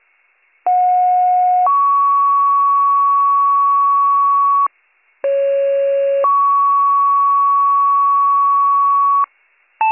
Two-tone page, heard as a one-second tone, then a three-second tone, about 730 then 1120 Hz, about 550 then 1120 Hz. It is heard from 0.7 to 9.2 s.